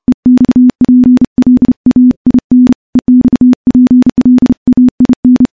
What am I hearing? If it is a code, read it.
Morse code, heard as a keyed tone, about 31 wpm, 250 Hz. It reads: EXPLAIN EXPLAIN
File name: morse / morse-training-30WPM